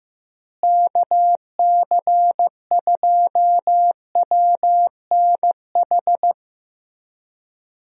Morse code, heard as a keyed tone, about 15 wpm, 700 Hz. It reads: KC2WNH
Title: USA Call Signs - 6 Characters - Repeated 3x - 15wpm